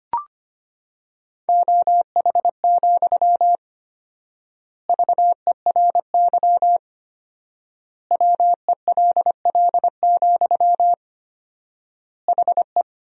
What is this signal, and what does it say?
Morse code, heard as a keyed tone, about 25 wpm, 700 Hz. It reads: OH, VERY WELL, HE
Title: 4x Word-Spacing - Sentences from Top 500 Words - 25wpm